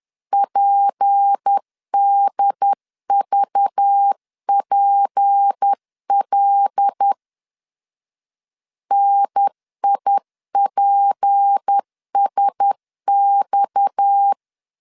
Morse code, heard as a keyed tone, about 10 wpm, 800 Hz. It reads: PDVPL NIPSX
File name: CW / DET / 1 6 1500C S C 10wpm 4252